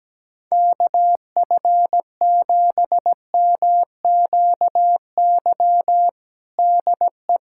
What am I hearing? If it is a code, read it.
Morse code, heard as a keyed tone, about 17 wpm, 700 Hz. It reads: KF7MQY DE
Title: USA Callsign DE Callsign - 6 Characters - 17wpm